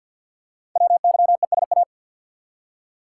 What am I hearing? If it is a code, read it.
Morse code, heard as a keyed tone, about 48 wpm, 700 Hz. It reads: WYESA
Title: Code Groups - 5 Letters - 50wpm